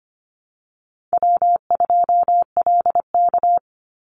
Morse code, heard as a keyed tone, about 25 wpm, 700 Hz. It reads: W2LK
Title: Call Signs Repeated 3x - 25wpm